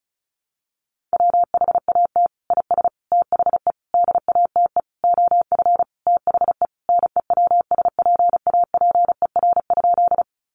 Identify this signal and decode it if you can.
Morse code, heard as a keyed tone, about 34 wpm, 700 Hz. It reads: WHAT IS THE DATE OF THE NEWSPAPER?
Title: Sentences from Top 1100 Words 35wpm